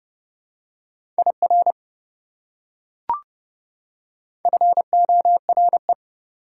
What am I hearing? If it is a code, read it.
Morse code, heard as a keyed tone, about 30 wpm, 700 Hz. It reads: IR FORE
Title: Prefixes and Suffixes 30wpm